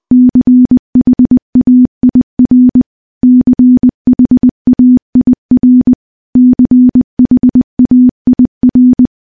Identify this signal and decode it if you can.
Morse code, heard as a keyed tone, about 20 wpm, 250 Hz. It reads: CHAIR CHAIR CHAIR
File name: morse / morse-training-20WPM